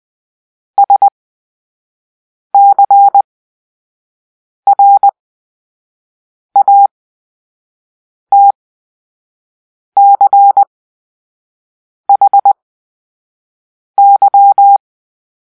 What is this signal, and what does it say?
Morse code, heard as a keyed tone, about 20 wpm, spaced as for 5 wpm, 800 Hz. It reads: SCRATCHY
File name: Grp-3 Drill-2